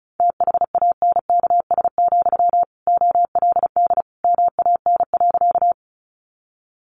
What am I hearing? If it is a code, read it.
Morse code, heard as a keyed tone, about 34 wpm, 700 Hz. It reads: THANKS, OLD MAN.
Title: Sentences from Top 600 Words 35wpm